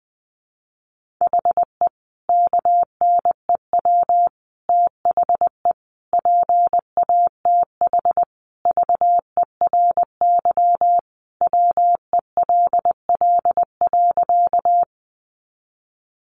Morse code, heard as a keyed tone, about 19 wpm, 700 Hz. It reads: HE KNEW THE PATH VERY WELL.
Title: Sentences from Top 1400 Words 20wpm